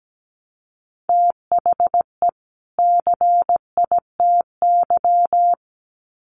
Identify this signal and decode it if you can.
Morse code, heard as a keyed tone, about 17 wpm, 700 Hz. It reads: THE CITY
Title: Sets of 2 Words 17wpm